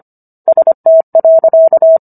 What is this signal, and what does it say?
Morse code, heard as a keyed tone, about 25 wpm, 650 Hz. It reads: ST.